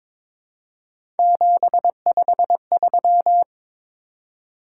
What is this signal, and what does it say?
Morse code, heard as a keyed tone, about 22 wpm, 700 Hz. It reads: 753
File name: practice / 22wpm / Three Letter-Number Combinations - 22wpm